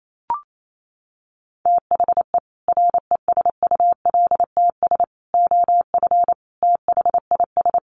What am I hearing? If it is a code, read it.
Morse code, heard as a keyed tone, about 28 wpm, 700 Hz. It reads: THE RESULTS OF THIS
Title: Sets of 5 Words 28wpm